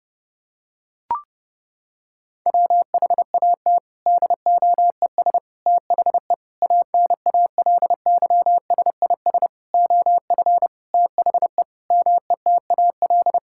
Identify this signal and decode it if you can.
Morse code, heard as a keyed tone, about 30 wpm, 700 Hz. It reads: WHAT DOES THE ANALYSIS OF THE METAL
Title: Sentences from Top 1000 Words - Review Part 6 - 30wpm